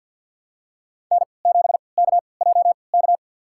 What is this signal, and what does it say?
Morse code, heard as a keyed tone, about 52 wpm, spaced as for 29 wpm, 700 Hz. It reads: N7KJK
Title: USA Call Signs - 5 Characters - 3x Character Spacing - 50wpm